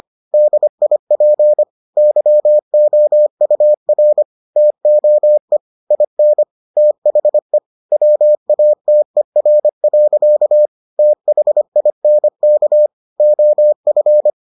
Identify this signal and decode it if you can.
Morse code, heard as a keyed tone, about 25 wpm, 600 Hz. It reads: DIP YOUR TOE IN THE WATER. THINK OF